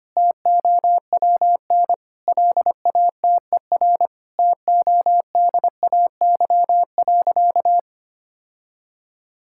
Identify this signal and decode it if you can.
Morse code, heard as a keyed tone, about 25 wpm, 700 Hz. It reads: TOWN LATER TODAY.
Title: Sentences from Top 1000 Words - Review Part 6 - 25wpm